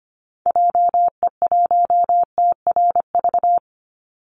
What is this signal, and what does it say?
Morse code, heard as a keyed tone, about 25 wpm, 700 Hz. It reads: JE1TRV
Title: Call Signs Repeated 3x - 25wpm